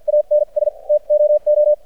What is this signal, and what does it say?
Morse code, heard as a keyed tone, about 52 wpm, spaced as for 41 wpm, 600 Hz. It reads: ANS TOO
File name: archived / CRSnet